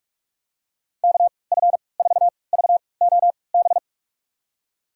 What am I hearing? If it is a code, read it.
Morse code, heard as a keyed tone, about 46 wpm, spaced as for 25 wpm, 700 Hz. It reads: KF4VOB
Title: USA Call Signs - 6 Characters - 3x Character Spacing - 45wpm